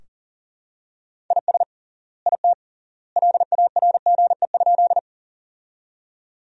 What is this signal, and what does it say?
Morse code, heard as a keyed tone, about 37 wpm, 700 Hz. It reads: IS IT LARGE?
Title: 3x Word-Spacing - Sentences from Top 200 Words - 40wpm